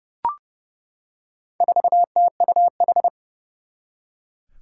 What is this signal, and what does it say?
Morse code, heard as a keyed tone, about 30 wpm, 700 Hz. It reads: 4TUH
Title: Four Challenging Letter-Number Combinations 30wpm